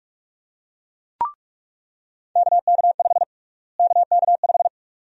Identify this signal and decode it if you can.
Morse code, heard as a keyed tone, about 46 wpm, 700 Hz. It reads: KK5 KK5
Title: USA Call Sign Prefixes - Repeated 2x - 45wpm